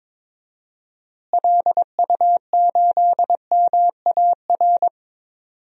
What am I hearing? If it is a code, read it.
Morse code, heard as a keyed tone, about 22 wpm, 700 Hz. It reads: LU8MAR
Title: DXCC-100 Call Signs - 6 Characters - Repeated 2x - 22wpm